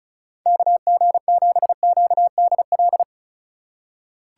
Morse code, heard as a keyed tone, about 36 wpm, 690 Hz. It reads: KG7QDL
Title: USA Call Signs - 6 Characters - 6x Character Spacing - 35wpm